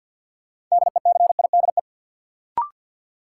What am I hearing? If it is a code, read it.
Morse code, heard as a keyed tone, about 53 wpm, 700 Hz. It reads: DECIDE
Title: Sentences from Top 500 Words - New Word Drill - 50wpm